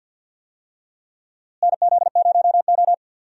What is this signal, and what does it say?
Morse code, heard as a keyed tone, about 53 wpm, 700 Hz. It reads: NZ0O
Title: USA Call Signs - 4 Characters - Repeated 2x - 50wpm